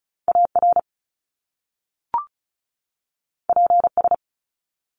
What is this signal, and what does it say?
Morse code, heard as a keyed tone, about 36 wpm, 700 Hz. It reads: AR PS
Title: Top Two Letter Combinations in Words 35wpm